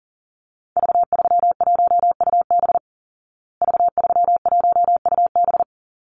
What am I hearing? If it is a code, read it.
Morse code, heard as a keyed tone, about 40 wpm, 700 Hz. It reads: V31UB V31UB
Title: Call Signs Repeated 2x - 40wpm